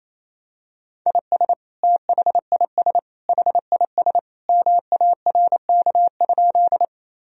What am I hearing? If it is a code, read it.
Morse code, heard as a keyed tone, about 28 wpm, 700 Hz. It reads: IS THIS HIS MARK?
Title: Sentences from Top 1000 Words - Review Part 1 - 28wpm